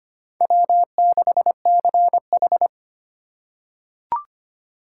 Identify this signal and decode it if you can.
Morse code, heard as a keyed tone, about 25 wpm, 700 Hz. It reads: W6CH
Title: USA Call Signs - 4 Characters - 4x Character Spacing - 25wpm